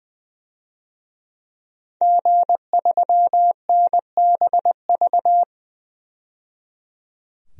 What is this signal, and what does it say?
Morse code, heard as a keyed tone, about 20 wpm, 700 Hz. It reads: G3NBV
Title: DXCC-100 Call Signs - 5 Characters - 20wpm